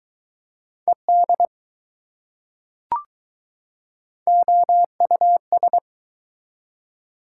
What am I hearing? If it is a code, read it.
Morse code, heard as a keyed tone, about 22 wpm, 700 Hz. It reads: ED OUS